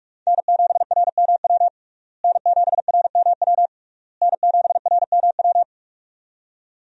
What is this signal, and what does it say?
Morse code, heard as a keyed tone, about 46 wpm, 700 Hz. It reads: N7RMW N7RMW N7RMW
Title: USA Call Signs - 5 Characters - Repeated 3x - 45wpm